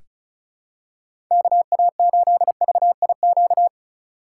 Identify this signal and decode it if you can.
Morse code, heard as a keyed tone, about 34 wpm, 700 Hz. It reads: KA8VIQ
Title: USA Call Signs - 6 Characters - 3x Character Spacing - 35wpm